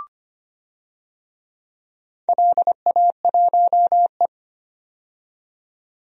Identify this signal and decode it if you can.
Morse code, heard as a keyed tone, about 25 wpm, 700 Hz. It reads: LA1E
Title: DXCC-100 Call Signs - 4 Characters - 25wpm